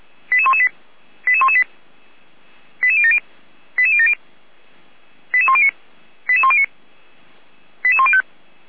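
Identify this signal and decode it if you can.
Five-tone selective call (ZVEI), heard as quick runs of five tones, tones of about 70 ms. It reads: 88118 88118 88070 88070 88109 88109 88105
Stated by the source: ZVEI five-tone selective call received on SDR.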